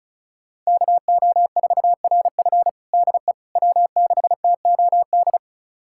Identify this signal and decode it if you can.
Morse code, heard as a keyed tone, about 36 wpm, 700 Hz. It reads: KO4RF DE W6TOD
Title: USA Callsign DE Callsign - 5 Characters - 35wpm